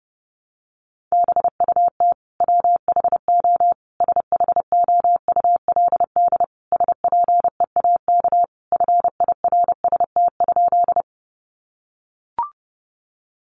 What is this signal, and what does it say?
Morse code, heard as a keyed tone, about 30 wpm, 700 Hz. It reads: BUT WHO SHOULD SPEAK FIRST?
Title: Sentences from Top 1100 Words 30wpm